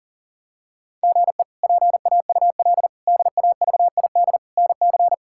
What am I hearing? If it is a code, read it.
Morse code, heard as a keyed tone, about 40 wpm, 690 Hz. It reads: GE PAUL DAVID NC
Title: K1USN SST QSO Practice - 40wpm